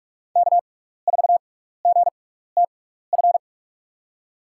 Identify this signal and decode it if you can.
Morse code, heard as a keyed tone, about 46 wpm, 700 Hz. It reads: K 4 G T F
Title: USA Call Signs - 5 Characters - 6x Character Spacing - 45wpm